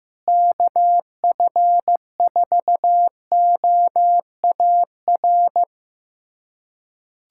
Morse code, heard as a keyed tone, about 15 wpm, 700 Hz. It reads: KF4OAR
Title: USA Callsign DE Callsign - 6 Characters - 15wpm